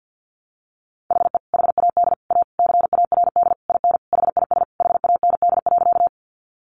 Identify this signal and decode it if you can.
Morse code, heard as a keyed tone, about 48 wpm, 700 Hz. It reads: HE HAD A CARD IN HIS HAND.
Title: Sentences from Top 600 Words 50wpm